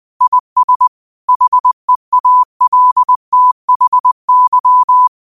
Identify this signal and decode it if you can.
Morse code, heard as a keyed tone, about 20 wpm, 1000 Hz. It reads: IS HEALTHY
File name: qotdfiles / 20wpm